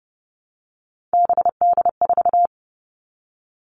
Morse code, heard as a keyed tone, about 30 wpm, 700 Hz. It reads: BD4
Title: Callsign Prefixes for Top 100 DXCC - 30wpm